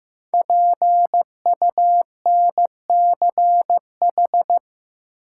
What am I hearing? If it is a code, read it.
Morse code, heard as a keyed tone, about 15 wpm, 700 Hz. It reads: PUNCH
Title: Five Letter Words - 15wpm